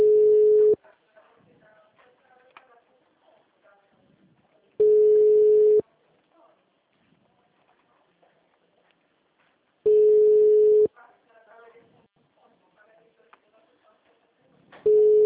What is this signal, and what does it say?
Ringback tone (Europe, 425 Hz), with speech.
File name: out- -2015- - - .9246